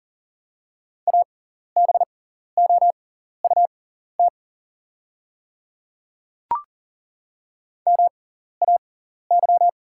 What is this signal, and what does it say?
Morse code, heard as a keyed tone, about 41 wpm, spaced as for 14 wpm, 690 Hz. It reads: ABOUT MAY
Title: 6x Character Spacing - Top 100 Words - 40wpm